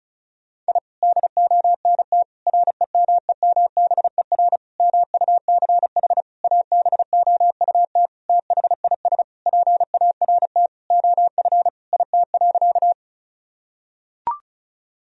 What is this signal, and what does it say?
Morse code, heard as a keyed tone, about 36 wpm, 700 Hz. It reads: I DONT REMEMBER MUCH ABOUT THIS PART OF IT.
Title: Sentences from Top 1000 Words - Review Part 1 - 35wpm